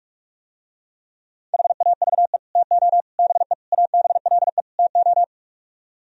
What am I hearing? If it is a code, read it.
Morse code, heard as a keyed tone, about 48 wpm, 700 Hz. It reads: HAVE TO BE ABLE TO